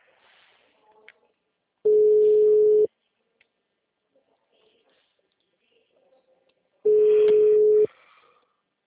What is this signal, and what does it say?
Ringback tone (Europe, 425 Hz).
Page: Rio Claro